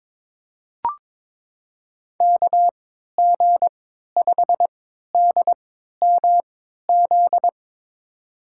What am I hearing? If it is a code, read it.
Morse code, heard as a keyed tone, about 22 wpm, spaced as for 12 wpm, 700 Hz. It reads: KG5DMZ
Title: USA Call Signs - 6 Characters - 3x Character Spacing - 22wpm